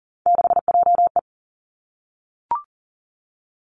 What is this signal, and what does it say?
Morse code, heard as a keyed tone, about 38 wpm, 700 Hz. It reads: 6JE
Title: Three Challenging Letter-Number Combinations 40wpm